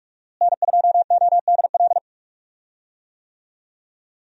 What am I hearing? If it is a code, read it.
Morse code, heard as a keyed tone, about 43 wpm, 700 Hz. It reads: N2ODL
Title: USA Call Signs - 5 Characters - Repeated 2x - 45wpm